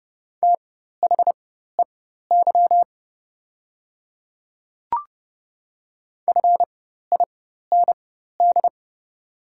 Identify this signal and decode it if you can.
Morse code, heard as a keyed tone, about 30 wpm, spaced as for 14 wpm, 700 Hz. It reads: THEY FIND
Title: 4x Character Spacing - Top 100 Words - 30wpm